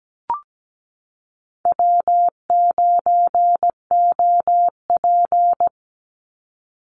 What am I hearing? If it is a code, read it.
Morse code, heard as a keyed tone, about 17 wpm, 700 Hz. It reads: W9OP
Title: Call Signs Repeated 3x - 17wpm